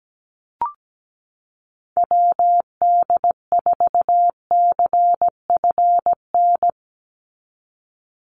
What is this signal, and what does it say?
Morse code, heard as a keyed tone, about 17 wpm, 700 Hz. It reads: WD4CFN
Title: NA SOTA Chaser Callsigns 17wpm